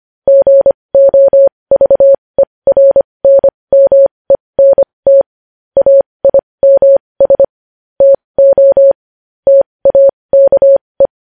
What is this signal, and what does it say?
Morse code, heard as a keyed tone, about 25 wpm, spaced as for 20 wpm, 550 Hz. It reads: GOVERNMENT AIMS TO TAKE
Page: news headlines in morse code at 20 wpm